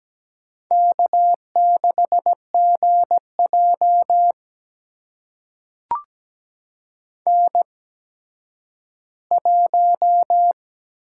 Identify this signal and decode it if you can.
Morse code, heard as a keyed tone, about 17 wpm, spaced as for 10 wpm, 700 Hz. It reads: K6GJ N1
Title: USA Call Signs - 4 Characters - 8x Character Spacing - 17wpm